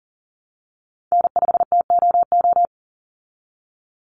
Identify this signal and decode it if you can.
Morse code, heard as a keyed tone, about 40 wpm, 700 Hz. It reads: N5TOO
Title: Call Signs Repeated 2x - 40wpm